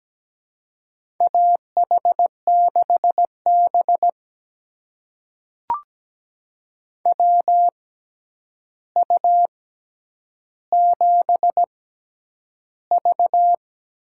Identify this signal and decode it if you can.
Morse code, heard as a keyed tone, about 17 wpm, spaced as for 9 wpm, 700 Hz. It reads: AH6B WU7V
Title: USA Call Signs - 4 Characters - 6x Character Spacing - 17wpm